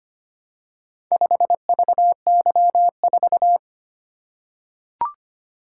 Morse code, heard as a keyed tone, about 25 wpm, 700 Hz. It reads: 5VY4
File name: practice / 25wpm / Four Letter-Number Combinations - 25wpm